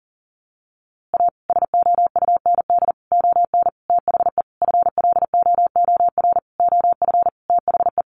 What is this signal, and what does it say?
Morse code, heard as a keyed tone, about 37 wpm, 700 Hz. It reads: A SOUND ON THE FLOOR OF THE
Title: A Sound Of Thunder 40wpm